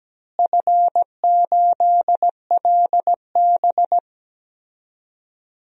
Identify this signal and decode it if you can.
Morse code, heard as a keyed tone, about 17 wpm, 700 Hz. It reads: F8LB
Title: DXCC-100 Call Signs - 4 Characters - Repeated 2x - 17wpm